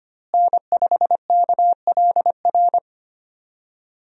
Morse code, heard as a keyed tone, about 25 wpm, 700 Hz. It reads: N5KLR
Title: USA Call Signs - 5 Characters - Repeated 3x - 25wpm